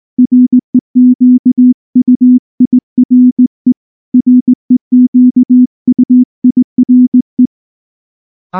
Morse code, heard as a keyed tone, about 18 wpm, 250 Hz. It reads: REQUIRE REQUIRE
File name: morse / morse-training-20WPM-6